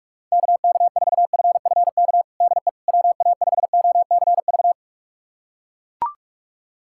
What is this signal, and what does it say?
Morse code, heard as a keyed tone, about 42 wpm, 700 Hz. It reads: KK4FFK DE WA5OCV
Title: USA Callsign DE Callsign - 6 Characters - 45wpm